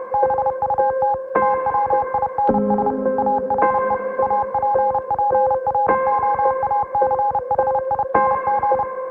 Morse code, heard as a keyed tone, about 31 wpm, 820 Hz. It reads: BUT TWINS ARE APPROACHING